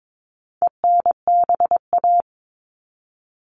Morse code, heard as a keyed tone, about 22 wpm, 700 Hz. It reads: ENBA
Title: All Cut Numbers - 4-Digit 22wpm